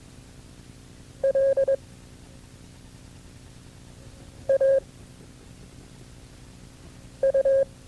Morse code, heard as a keyed tone, about 21 wpm, 540 Hz. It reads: L A U
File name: FreeFiles / CW-practice / KB6MT-tape2-side1